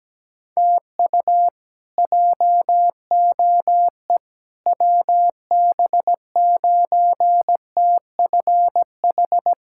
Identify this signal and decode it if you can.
Morse code, heard as a keyed tone, about 17 wpm, 700 Hz. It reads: TU JOE WB9TFH
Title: K1USN SST QSO Practice - 17wpm